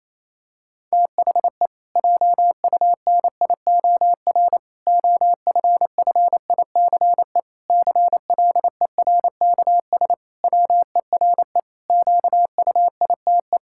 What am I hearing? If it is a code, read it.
Morse code, heard as a keyed tone, about 27 wpm, 700 Hz. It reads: THE JUNIOR OFFICE CLERKS WERE QUITE
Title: Pangram Challenge v2 - 28wpm